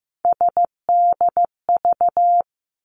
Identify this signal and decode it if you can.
Morse code, heard as a keyed tone, about 15 wpm, 700 Hz. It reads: SDV
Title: Three Challenging Letter-Number Combinations 15wpm